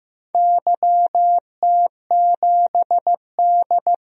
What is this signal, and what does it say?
Morse code, heard as a keyed tone, about 15 wpm, 700 Hz. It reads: YT7D
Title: DXCC-100 Call Signs - 4 Characters - 15wpm